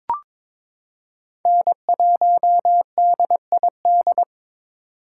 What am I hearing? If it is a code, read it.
Morse code, heard as a keyed tone, about 21 wpm, 700 Hz. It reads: N1DID
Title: K1USN SST QSO Practice - 22wpm